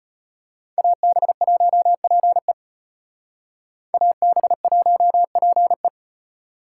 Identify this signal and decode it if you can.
Morse code, heard as a keyed tone, about 32 wpm, 700 Hz. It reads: AB1PE AB1PE